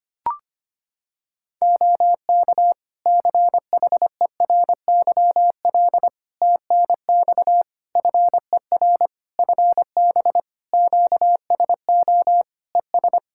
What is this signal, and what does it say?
Morse code, heard as a keyed tone, about 25 wpm, 700 Hz. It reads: OK CHERYL TNX FER FB QSO ES
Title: Practice QSO - Exchange 4 - Ending - 25wpm